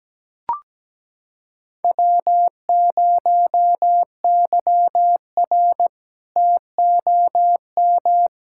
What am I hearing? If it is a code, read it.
Morse code, heard as a keyed tone, about 17 wpm, 700 Hz. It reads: W0YR TOM
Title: K1USN SST QSO Practice - 17wpm